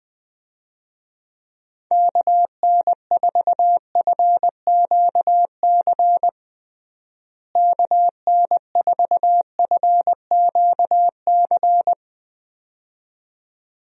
Morse code, heard as a keyed tone, about 21 wpm, 700 Hz. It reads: KN4FQC KN4FQC
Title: USA Call Signs - 6 Characters - Repeated 2x - 20wpm